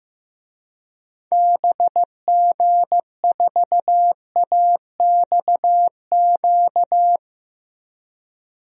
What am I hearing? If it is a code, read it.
Morse code, heard as a keyed tone, about 15 wpm, 700 Hz. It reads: BG4AXQ
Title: DXCC-100 Call Signs - 6 Characters - Repeated 2x - 15wpm